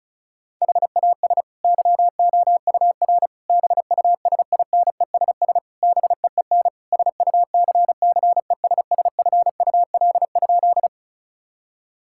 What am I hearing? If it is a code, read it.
Morse code, heard as a keyed tone, about 36 wpm, 700 Hz. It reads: HAS YOUR BUSINESS BEEN SUCCESSFUL?
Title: Sentences from Top 1000 Words - Review Part 6 - 35wpm